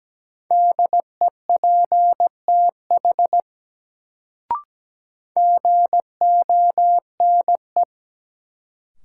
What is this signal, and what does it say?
Morse code, heard as a keyed tone, about 17 wpm, 700 Hz. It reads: DEPTH GONE
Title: Sentences from Top 1200 Words - New Word Drill Alt - 17wpm